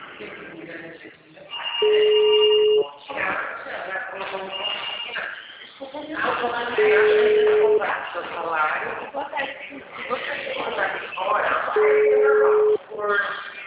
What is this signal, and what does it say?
Ringback tone (Europe, 425 Hz), with speech.